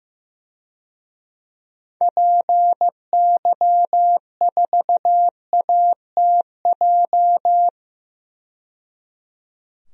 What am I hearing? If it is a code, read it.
Morse code, heard as a keyed tone, about 15 wpm, 700 Hz. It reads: PY4ATJ
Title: DXCC-100 Call Signs - 6 Characters - 15wpm